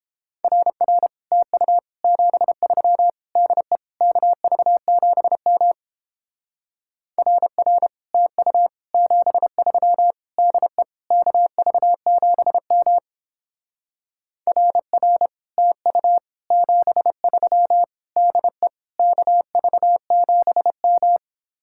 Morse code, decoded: RR TU 73 DE KV7M RR TU 73 DE KV7M RR TU 73 DE KV7M

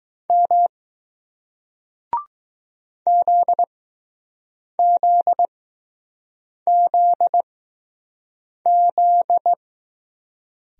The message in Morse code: M ZZZZ

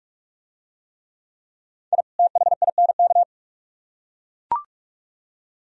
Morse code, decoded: I THINK

